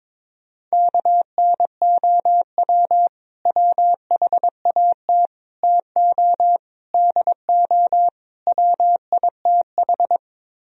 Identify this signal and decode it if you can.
Morse code, heard as a keyed tone, about 22 wpm, 700 Hz. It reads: KNOW WHAT TO DO WITH
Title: Speed-Racing - Sets of 5 Words 22wpm